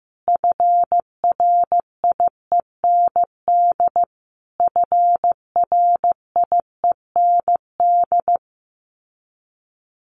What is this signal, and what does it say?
Morse code, heard as a keyed tone, about 15 wpm, 700 Hz. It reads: FRIEND FRIEND
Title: Top 500 Words Repeated 15wpm